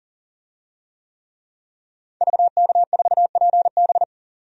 Morse code, heard as a keyed tone, about 40 wpm, 700 Hz. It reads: VK4PB
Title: DXCC-100 Call Signs - 5 Characters - 40wpm